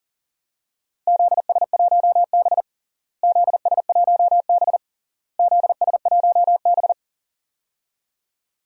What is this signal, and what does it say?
Morse code, heard as a keyed tone, about 40 wpm, 700 Hz. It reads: ZS1B ZS1B ZS1B
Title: DXCC-100 Call Signs - 4 Characters - Repeated 3x - 40wpm